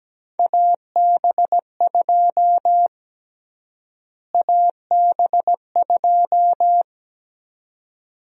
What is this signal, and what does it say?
Morse code, heard as a keyed tone, about 17 wpm, 700 Hz. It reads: AB2 AB2
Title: USA Call Sign Prefixes - Repeated 2x - 17wpm